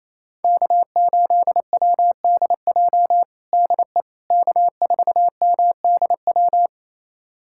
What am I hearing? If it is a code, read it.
Morse code, heard as a keyed tone, about 28 wpm, 700 Hz. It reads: K8WDJ DE K4MDW
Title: USA Callsign DE Callsign - 5 Characters - 28wpm